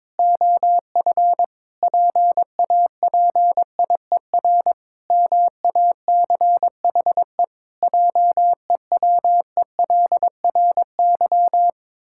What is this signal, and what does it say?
Morse code, heard as a keyed tone, about 22 wpm, 700 Hz. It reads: OF PAPIER MACHE JEWELRY
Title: Pangram Challenge v2 - 22wpm